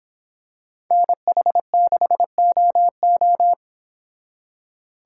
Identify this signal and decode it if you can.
Morse code, heard as a keyed tone, about 26 wpm, 700 Hz. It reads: NH6OO